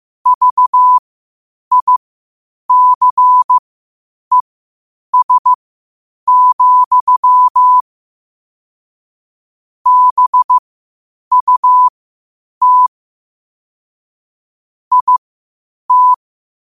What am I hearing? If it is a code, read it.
Morse code, heard as a keyed tone, about 15 wpm, spaced as for 8 wpm, 1000 Hz. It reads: VICES, BUT IT
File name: qotdfiles / 8wpm